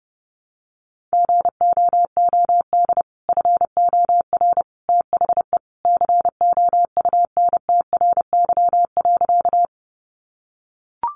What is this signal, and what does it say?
Morse code, heard as a keyed tone, about 30 wpm, 700 Hz. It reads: GOOD FOR THE COUNTRY.
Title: Sentences from Top 300 Words 30wpm